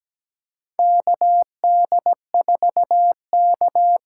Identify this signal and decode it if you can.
Morse code, heard as a keyed tone, about 17 wpm, 700 Hz. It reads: KD4K
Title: USA Call Signs - 6 Characters - 8x Character Spacing - 17wpm